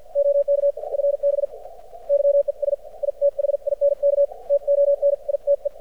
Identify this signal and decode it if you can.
Morse code, heard as a keyed tone, about 54 wpm, spaced as for 45 wpm, 590 Hz. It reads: OK FB YES I THINK TONITE
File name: archived / CRSnet